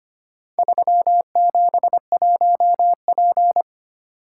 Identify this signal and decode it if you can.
Morse code, heard as a keyed tone, about 25 wpm, 700 Hz. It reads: 371P
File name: practice / 25wpm / Four Letter-Number Combinations - 25wpm